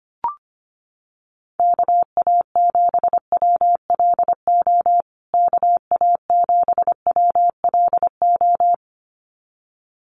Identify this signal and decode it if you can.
Morse code, heard as a keyed tone, about 25 wpm, 700 Hz. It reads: KA7WLO KA7WLO
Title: USA Call Signs Repeated 2x 25wpm